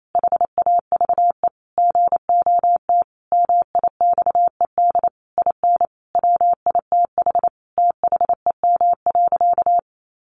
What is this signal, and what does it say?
Morse code, heard as a keyed tone, about 28 wpm, 700 Hz. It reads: HAVE GOT MIXED IN WITH THEM.
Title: Sentences from Top 1500 Words 28wpm